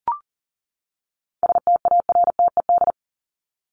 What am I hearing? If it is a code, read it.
Morse code, heard as a keyed tone, about 40 wpm, 700 Hz. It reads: STARTED
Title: ED Suffix - Top 100 Words with Base Words 40wpm